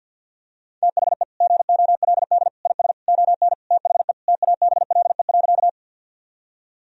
Morse code, heard as a keyed tone, about 52 wpm, 700 Hz. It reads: THE GOLD IS ON THE TABLE.
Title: Sentences from Top 1000 Words - Review Part 3 - 50wpm